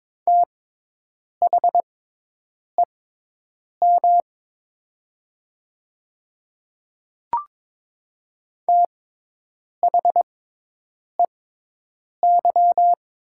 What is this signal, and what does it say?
Morse code, heard as a keyed tone, about 21 wpm, spaced as for 8 wpm, 700 Hz. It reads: THEM THEY
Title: 6x Character Spacing - Top 100 Words - 22wpm